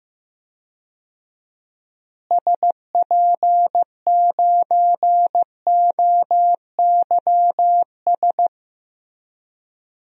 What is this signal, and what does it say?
Morse code, heard as a keyed tone, about 15 wpm, 700 Hz. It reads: SP9OYS
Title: DXCC-100 Call Signs - 6 Characters - Repeated 2x - 15wpm